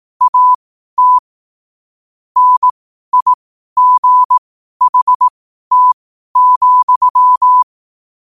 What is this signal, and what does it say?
Morse code, heard as a keyed tone, about 18 wpm, spaced as for 12 wpm, 1000 Hz. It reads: AT NIGHT,
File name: qotdfiles / 12wpm